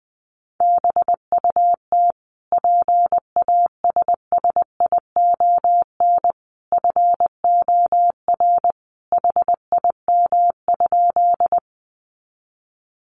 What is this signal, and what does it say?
Morse code, heard as a keyed tone, about 20 wpm, 700 Hz. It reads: BUT PASSION FOR HIM?
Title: Sentences from Top 1300 Words 20wpm